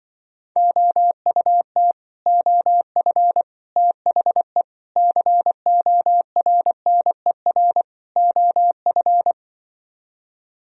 Morse code, heard as a keyed tone, about 24 wpm, 700 Hz. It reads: OUT OF THE CORNER OF
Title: Speed-Racing - Sets of 5 Words 20wpm